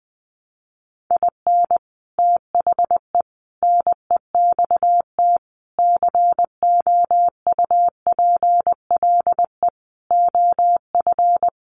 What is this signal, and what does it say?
Morse code, heard as a keyed tone, about 20 wpm, 700 Hz. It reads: IN THE NEXT COUPLE OF
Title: Sets of 5 Words Encore2 20wpm